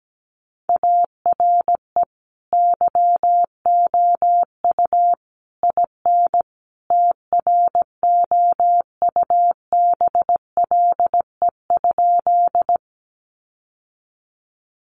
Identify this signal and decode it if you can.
Morse code, heard as a keyed tone, about 17 wpm, 700 Hz. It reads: ARE YOU IN TROUBLE?
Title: Sentences from Top 1200 Words 17wpm